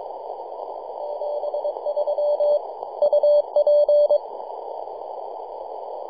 Morse code, heard as a keyed tone, about 25 wpm, 590 Hz. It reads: 73 UP
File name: 3d2ya-18-085-cw